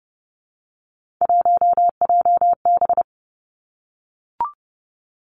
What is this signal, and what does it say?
Morse code, heard as a keyed tone, about 30 wpm, 700 Hz. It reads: 1JB
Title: Three Challenging Letter-Number Combinations 30wpm